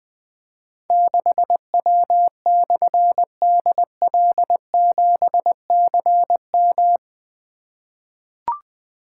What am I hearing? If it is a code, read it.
Morse code, heard as a keyed tone, about 20 wpm, 700 Hz. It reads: 6W/DL7CM
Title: Difficult Call Signs 20wpm